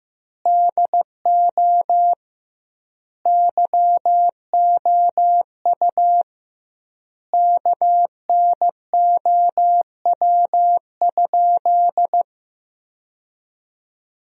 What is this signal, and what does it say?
Morse code, heard as a keyed tone, about 16 wpm, 700 Hz. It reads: DO YOU KNOW?
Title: 2x Word-Spacing - Sentences from Top 100 Words - 15wpm